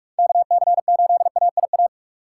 Morse code, heard as a keyed tone, about 46 wpm, 700 Hz. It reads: KC8AIA
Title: USA Call Signs - 6 Characters - 3x Character Spacing - 45wpm